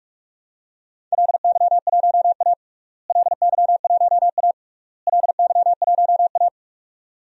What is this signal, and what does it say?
Morse code, heard as a keyed tone, about 46 wpm, 700 Hz. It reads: LY1A LY1A LY1A
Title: DXCC-100 Call Signs - 4 Characters - Repeated 3x - 45wpm